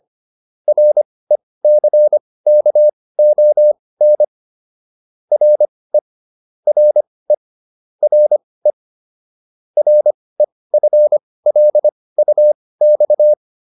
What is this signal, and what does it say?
Morse code, heard as a keyed tone, about 25 wpm, spaced as for 18 wpm, 600 Hz. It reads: RECKON RE RE RE REFLUX